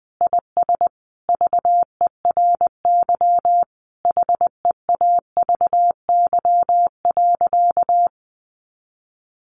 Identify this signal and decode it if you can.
Morse code, heard as a keyed tone, about 20 wpm, 700 Hz. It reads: IS VERY HEAVY.
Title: Sentences from Top 1100 Words 20wpm